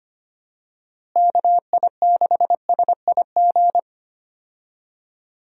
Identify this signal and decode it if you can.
Morse code, heard as a keyed tone, about 25 wpm, 700 Hz. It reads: KI6SIG
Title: USA Call Signs - 6 Characters - Repeated 3x - 25wpm